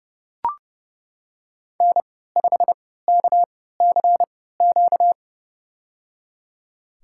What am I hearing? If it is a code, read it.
Morse code, heard as a keyed tone, about 32 wpm, spaced as for 17 wpm, 700 Hz. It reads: N5KCQ
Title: USA Call Signs - 3x Character Spacing - 30wpm